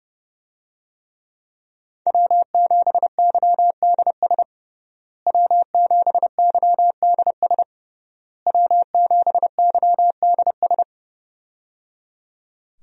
Morse code, heard as a keyed tone, about 29 wpm, 700 Hz. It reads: W7YDS W7YDS W7YDS
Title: USA Call Signs - 5 Characters - Repeated 3x - 30wpm